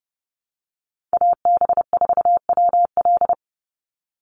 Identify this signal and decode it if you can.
Morse code, heard as a keyed tone, about 30 wpm, 700 Hz. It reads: AB4WL
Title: NA SOTA Chaser Callsigns 30wpm